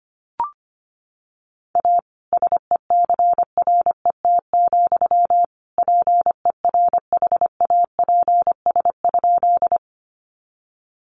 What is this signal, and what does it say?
Morse code, heard as a keyed tone, about 25 wpm, 700 Hz. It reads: A SECRET, PERHAPS?
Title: Sentences from Top 1200 Words 25wpm